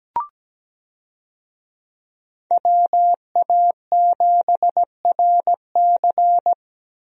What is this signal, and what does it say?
Morse code, heard as a keyed tone, about 17 wpm, 700 Hz. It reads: WA7RC
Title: USA Call Signs - 5 Characters - 17wpm